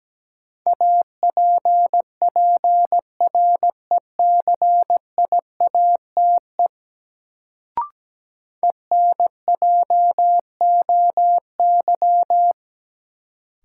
Morse code, heard as a keyed tone, about 17 wpm, 700 Hz. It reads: APPRECIATE ENJOY
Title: Sentences from Top 1300 Words - New Word Drill Alt - 17wpm